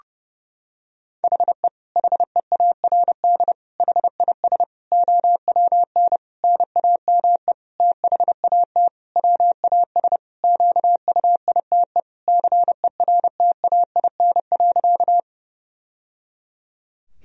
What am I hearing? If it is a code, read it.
Morse code, heard as a keyed tone, about 30 wpm, 700 Hz. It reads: HE HEARD HIS OWN NAME THAT WAS QUITE CERTAIN.
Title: Sentences from Top 1000 Words - Review Part 2 - 30wpm